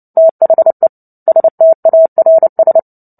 Morse code, heard as a keyed tone, about 30 wpm, 650 Hz. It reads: THE STARS
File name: Book Two / 03-Chapter 3 - 29 WPM